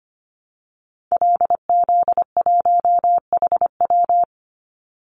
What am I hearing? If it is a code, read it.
Morse code, heard as a keyed tone, about 25 wpm, 700 Hz. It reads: LZ1HW
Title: Call Signs Repeated 3x - 25wpm